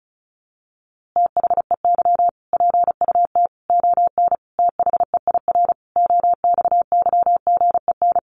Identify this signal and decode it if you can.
Morse code, heard as a keyed tone, about 34 wpm, 700 Hz. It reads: THEY PUT ON THEIR OXYGEN
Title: A Sound Of Thunder 35wpm